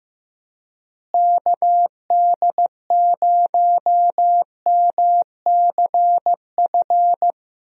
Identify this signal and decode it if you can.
Morse code, heard as a keyed tone, about 16 wpm, 700 Hz. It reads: KD0MCF
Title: USA Call Signs - 6 Characters - Repeated 2x - 15wpm